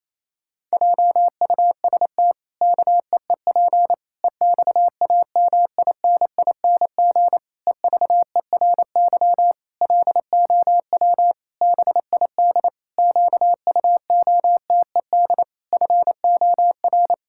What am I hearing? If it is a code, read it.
Morse code, heard as a keyed tone, about 28 wpm, 700 Hz. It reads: JUST KEEP EXAMINING EVERY LOW BID QUOTED FOR
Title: Pangram Challenge v2 - 28wpm